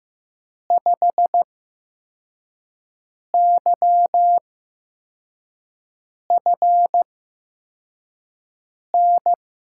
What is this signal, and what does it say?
Morse code, heard as a keyed tone, about 16 wpm, spaced as for 4 wpm, 700 Hz. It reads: 5YFN
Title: USA Call Signs - 6 Characters - 8x Character Spacing - 15wpm